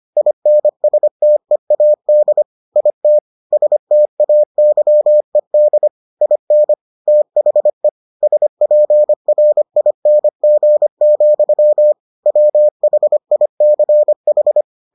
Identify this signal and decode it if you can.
Morse code, heard as a keyed tone, about 25 wpm, 600 Hz. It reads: INSTEAD IT STAYED IN THE SPRING, WHICH